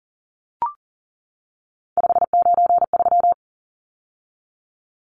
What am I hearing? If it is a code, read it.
Morse code, heard as a keyed tone, about 40 wpm, 700 Hz. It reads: 593
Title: Numbers - 3-Digit 40wpm